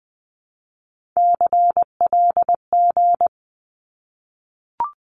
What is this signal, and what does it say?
Morse code, heard as a keyed tone, about 20 wpm, 700 Hz. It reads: CLG
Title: Abbreviations 20wpm